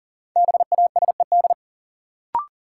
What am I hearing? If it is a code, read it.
Morse code, heard as a keyed tone, about 37 wpm, 700 Hz. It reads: BASED
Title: Sets of 2 Words - New Word Drill - 40wpm